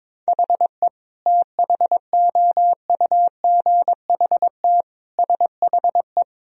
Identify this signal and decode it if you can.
Morse code, heard as a keyed tone, about 22 wpm, 700 Hz. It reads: HE THOUGHT SHE
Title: Sentences from Top 1000 Words - Review Part 1 - 22wpm